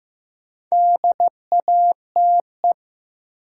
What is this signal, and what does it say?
Morse code, heard as a keyed tone, about 16 wpm, 700 Hz. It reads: DATE
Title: Sentences from Top 1000 Words - Review Part 7 - New Word Drill - 15wpm